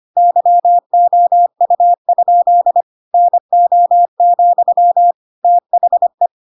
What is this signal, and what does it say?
Morse code, heard as a keyed tone, about 25 wpm, 700 Hz. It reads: YOU? NO, THE
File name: Pinocchio / Pinocchio ch5 25 25 ik1hge